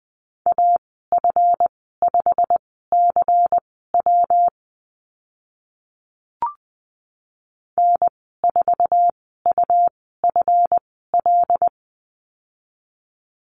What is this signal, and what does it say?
Morse code, heard as a keyed tone, about 20 wpm, spaced as for 14 wpm, 700 Hz. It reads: AF5CW N4UFL
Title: 140 Lesson 23g - Letter f - Call Signs 20wpm